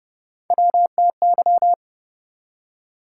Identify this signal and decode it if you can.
Morse code, heard as a keyed tone, about 36 wpm, 700 Hz. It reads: WTY